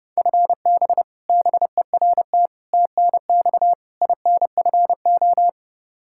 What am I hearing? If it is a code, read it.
Morse code, heard as a keyed tone, about 31 wpm, 700 Hz. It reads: FB BERT TNX INFO
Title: Practice QSO - Exchange 3 - More Info - 30wpm